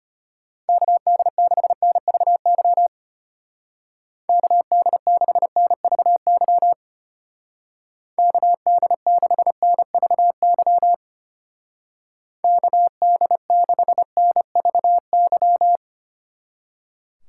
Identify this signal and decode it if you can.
Morse code, heard as a keyed tone, about 31 wpm, 700 Hz. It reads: KD6NVY KD6NVY KD6NVY KD6NVY
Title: Speed Racing - USA 2x3 Call Signs 25wpm